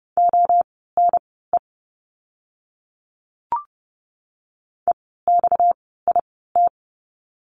Morse code, heard as a keyed tone, about 30 wpm, spaced as for 17 wpm, 700 Hz. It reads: ONE EXIT
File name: course / 30:10wpm / 230 Lesson 37f - Letter x - Words 30wpm